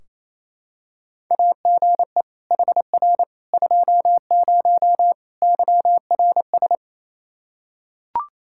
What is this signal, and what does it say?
Morse code, heard as a keyed tone, about 28 wpm, 700 Hz. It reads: AGE HR 20 YRS
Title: Practice QSO - Exchange 3 - More Info - 28wpm